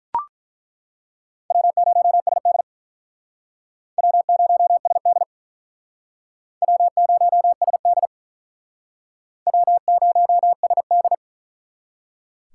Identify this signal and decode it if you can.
Morse code, heard as a keyed tone, about 48 wpm, 700 Hz. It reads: W0SD W0SD W0SD W0SD